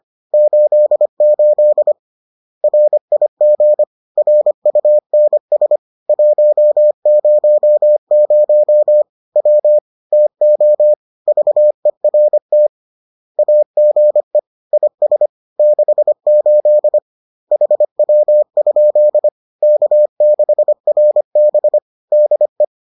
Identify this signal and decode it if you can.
Morse code, heard as a keyed tone, about 25 wpm, 600 Hz. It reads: 88 RIG RUNS 100 W TO VERT AGE IS 68 HW? K6RB DE